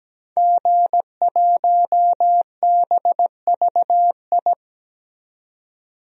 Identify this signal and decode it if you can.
Morse code, heard as a keyed tone, about 17 wpm, 700 Hz. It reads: G1BVI
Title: DXCC-100 Call Signs - 5 Characters - 17wpm